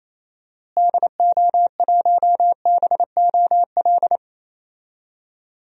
Morse code, heard as a keyed tone, about 28 wpm, 700 Hz. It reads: DO1BOL
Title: DXCC-100 Call Signs - 6 Characters - Repeated 2x - 28wpm